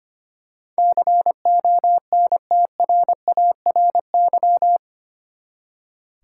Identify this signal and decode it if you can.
Morse code, heard as a keyed tone, about 25 wpm, 700 Hz. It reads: CONTRARY
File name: Eight Letter Words - 25wpm